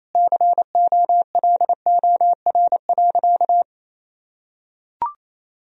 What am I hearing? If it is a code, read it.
Morse code, heard as a keyed tone, about 28 wpm, 700 Hz. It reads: COLOR.
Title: Sentences from Top 1000 Words - Review Part 2 - 28wpm